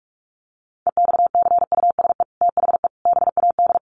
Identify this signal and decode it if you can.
Morse code, heard as a keyed tone, about 44 wpm, 700 Hz. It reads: EXCUSE THE BAD
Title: Sentences from Top 800 Words 45wpm